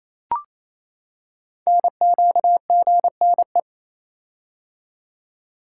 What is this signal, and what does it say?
Morse code, heard as a keyed tone, about 28 wpm, 700 Hz. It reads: NQGNE
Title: Code Groups - 5 Letters - 28wpm